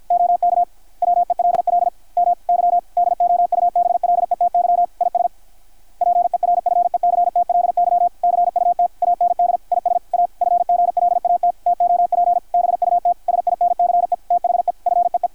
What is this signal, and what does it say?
Morse code, heard as a keyed tone, about 52 wpm, 660 Hz, with radio noise. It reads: OK WELL MY DOUBLETY IS PERFECTLY CUT AND IS A FGLAT TOP BUT SINCE THE FEE
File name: archived / CRSnet